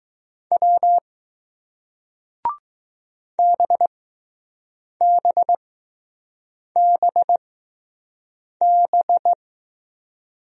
Morse code, heard as a keyed tone, about 20 wpm, spaced as for 6 wpm, 700 Hz. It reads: W BBBB